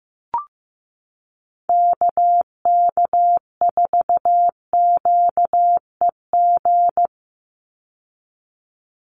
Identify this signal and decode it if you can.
Morse code, heard as a keyed tone, about 15 wpm, 700 Hz. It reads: KK4QEG
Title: USA Call Signs 15wpm